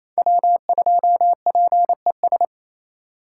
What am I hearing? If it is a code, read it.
Morse code, heard as a keyed tone, about 36 wpm, 700 Hz. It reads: W2PES